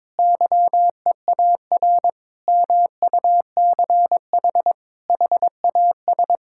Morse code, heard as a keyed tone, about 22 wpm, 700 Hz. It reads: YEAR MUCH HAS
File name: practice / 22wpm / Sentences with 3-Word Phrases - 22wpm